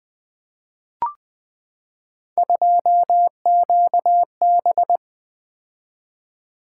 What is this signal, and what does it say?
Morse code, heard as a keyed tone, about 20 wpm, 700 Hz. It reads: 2QB
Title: Three Letter-Number Combinations - 20wpm